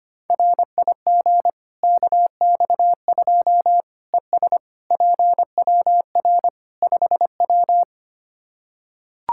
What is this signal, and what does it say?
Morse code, heard as a keyed tone, about 25 wpm, 700 Hz. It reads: RIG KX2 ES PWR 5W
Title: K1USN SST QSO Practice 25wpm